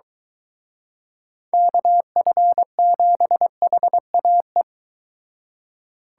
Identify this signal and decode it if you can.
Morse code, heard as a keyed tone, about 24 wpm, 700 Hz. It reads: KF7HAE